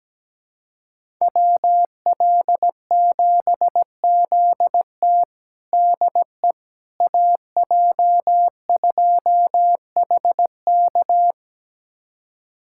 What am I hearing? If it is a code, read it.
Morse code, heard as a keyed tone, about 17 wpm, 700 Hz. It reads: WL7ZT DE AJ2HK
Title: USA Callsign DE Callsign - 5 Characters - 17wpm